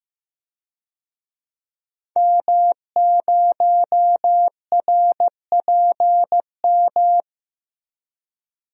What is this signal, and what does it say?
Morse code, heard as a keyed tone, about 15 wpm, 700 Hz. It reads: M0RPM
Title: DXCC-100 Call Signs - 5 Characters - Repeated 3x - 15wpm